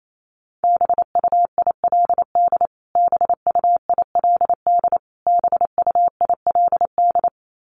Morse code, heard as a keyed tone, about 28 wpm, 700 Hz. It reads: BUILD BUILD BUILD
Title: Top 500 Words Repeated 28wpm